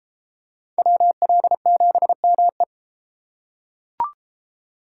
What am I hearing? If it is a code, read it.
Morse code, heard as a keyed tone, about 31 wpm, 700 Hz. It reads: WL7ME